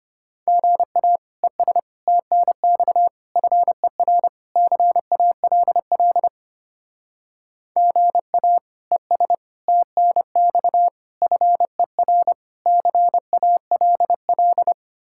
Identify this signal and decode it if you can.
Morse code, heard as a keyed tone, about 31 wpm, 700 Hz. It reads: GA ES TNX FER CALL GA ES TNX FER CALL
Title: Speed-Racing Common QSO Phrases - 25wpm